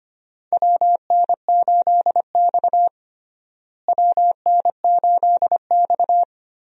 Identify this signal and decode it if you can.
Morse code, heard as a keyed tone, about 25 wpm, 700 Hz. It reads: WN8X WN8X
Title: USA Call Signs - 4 Characters - Repeated 3x - 25wpm